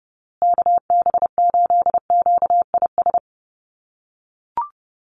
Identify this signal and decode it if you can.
Morse code, heard as a keyed tone, about 28 wpm, 700 Hz. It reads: KB8QIS
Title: USA Call Signs Repeated 3x 30wpm